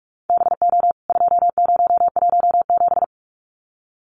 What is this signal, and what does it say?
Morse code, heard as a keyed tone, about 44 wpm, 700 Hz. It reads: BO 2017
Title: CWT Exchange 45wpm